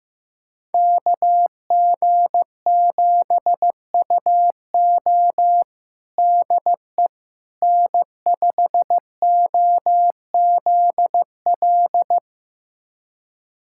Morse code, heard as a keyed tone, about 15 wpm, 700 Hz. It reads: KG7UO DE N5OZL
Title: USA Callsign DE Callsign - 5 Characters - 15wpm